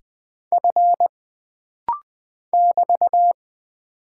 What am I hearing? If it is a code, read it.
Morse code, heard as a keyed tone, about 20 wpm, 700 Hz. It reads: F <BT>
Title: Single Character (Letter:Number:Punctuation:Prosign) - Rapid-Fire - 20wpm